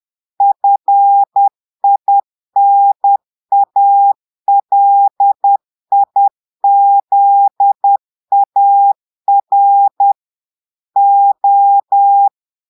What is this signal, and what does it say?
Morse code, heard as a keyed tone, about 10 wpm, 800 Hz. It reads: FINALIZAR O
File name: portugais texte en clair 2 vitesse 10 mots minutes